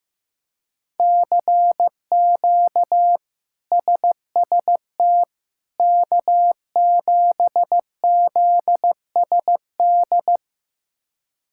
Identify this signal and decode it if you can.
Morse code, heard as a keyed tone, about 15 wpm, 700 Hz. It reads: CQ SST K7ZSD
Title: K1USN SST QSO Practice - 15wpm